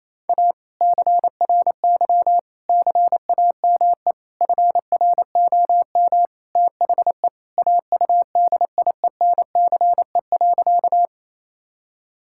Morse code, heard as a keyed tone, about 28 wpm, 700 Hz. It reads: A CRY CAME FROM THE AUDIENCE.
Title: Sentences from Top 1000 Words - Review Part 5 - 28wpm